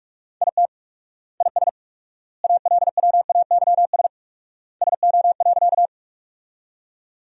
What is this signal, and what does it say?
Morse code, heard as a keyed tone, about 46 wpm, 700 Hz. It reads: IT IS ALWAYS SO.
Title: 4x Word-Spacing - Sentences from Top 300 Words - 45wpm